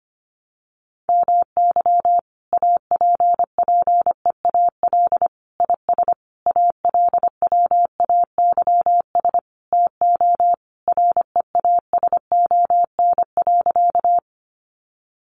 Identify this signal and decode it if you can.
Morse code, heard as a keyed tone, about 25 wpm, 700 Hz. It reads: MY APPEAL IS ALWAYS TO REASON.
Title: Sentences from Top 1300 Words 25wpm